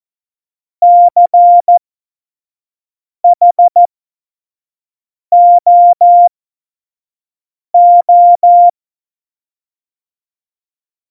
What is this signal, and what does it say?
Morse code, heard as a keyed tone, about 14 wpm, spaced as for 5 wpm, 700 Hz. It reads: CHOO
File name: Morsekurs / Witze / Witz 24. November 2020, 25 F70